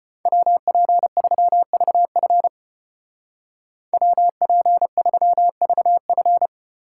Morse code, heard as a keyed tone, about 32 wpm, 700 Hz. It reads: WP3VF WP3VF